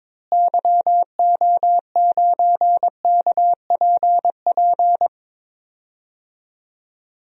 Morse code, heard as a keyed tone, about 22 wpm, 700 Hz. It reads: YO9KPP
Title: DXCC-100 Call Signs - 6 Characters - Repeated 2x - 22wpm